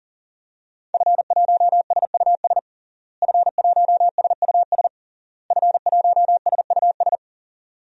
Morse code, heard as a keyed tone, about 37 wpm, 700 Hz. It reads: F1SUS F1SUS F1SUS
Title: DXCC-100 Call Signs - 5 Characters - Repeated 3x - 40wpm